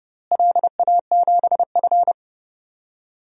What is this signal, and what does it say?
Morse code, heard as a keyed tone, about 31 wpm, 700 Hz. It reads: LA7F
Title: DXCC-100 Call Signs - 4 Characters - Repeated 3x - 30wpm